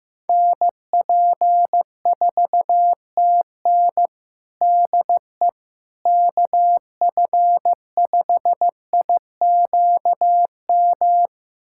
Morse code, heard as a keyed tone, about 15 wpm, 700 Hz. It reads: NP4TN DE KF5IQM
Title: USA Callsign DE Callsign 15wpm